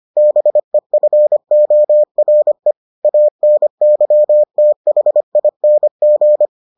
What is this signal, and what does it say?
Morse code, heard as a keyed tone, about 25 wpm, 600 Hz. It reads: BEFORE ANYTHING